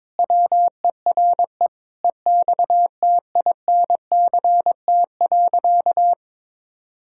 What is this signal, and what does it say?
Morse code, heard as a keyed tone, about 22 wpm, 700 Hz. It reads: WERE EXTINCT.
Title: Pangram Challenge v2 - 22wpm